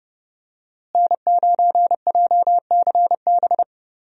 Morse code, heard as a keyed tone, about 31 wpm, 700 Hz. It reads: N9JCB
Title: USA Call Signs - 5 Characters - Repeated 3x - 30wpm